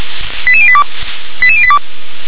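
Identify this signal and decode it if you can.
Five-tone selective call (ZVEI), heard as quick runs of five tones, tones of about 70 ms. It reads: 88082 88082
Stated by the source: ZVEI five-tone selective call received on an SDR.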